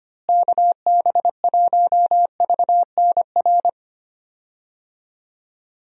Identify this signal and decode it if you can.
Morse code, heard as a keyed tone, about 25 wpm, 700 Hz. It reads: KB1VNR
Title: USA Call Signs - 6 Characters - Repeated 3x - 25wpm